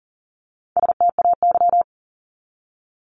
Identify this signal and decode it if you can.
Morse code, heard as a keyed tone, about 40 wpm, 700 Hz. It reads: STAY
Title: Top 500 Words 40wpm